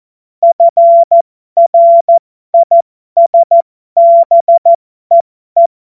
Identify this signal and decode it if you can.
Morse code, heard as a keyed tone, about 14 wpm, spaced as for 12 wpm, 680 Hz. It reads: FRISBEE
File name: Morsekurs / Witze